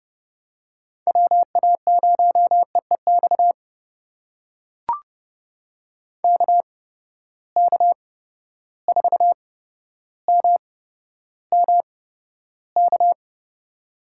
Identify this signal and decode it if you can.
Morse code, heard as a keyed tone, about 30 wpm, spaced as for 13 wpm, 700 Hz. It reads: WA0EEX KK4MMK
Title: USA Call Signs - 6 Characters - 8x Character Spacing - 30wpm